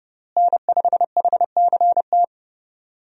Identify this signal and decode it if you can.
Morse code, heard as a keyed tone, about 32 wpm, 700 Hz. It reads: N5HCT
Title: USA Call Signs - 5 Characters - Repeated 3x - 30wpm